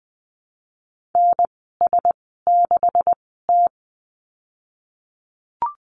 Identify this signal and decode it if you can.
Morse code, heard as a keyed tone, about 20 wpm, spaced as for 15 wpm, 700 Hz. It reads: NS6T
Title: 147 Lesson 24g - Letter y - Call Signs 20wpm